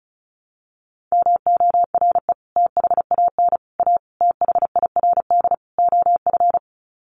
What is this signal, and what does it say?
Morse code, heard as a keyed tone, about 34 wpm, 700 Hz. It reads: MORE THAN A THIRD OF
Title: Sets of 5 Words Encore2 35wpm